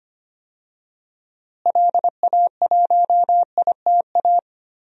Morse code, heard as a keyed tone, about 25 wpm, 700 Hz. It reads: LA1ITA
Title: DXCC-100 Call Signs - 6 Characters - Repeated 3x - 25wpm